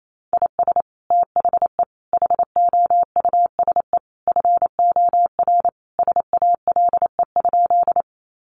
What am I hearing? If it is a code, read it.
Morse code, heard as a keyed tone, about 28 wpm, 700 Hz. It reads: IS THE HOUSE FOR SALE?
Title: Sentences from Top 1400 Words 28wpm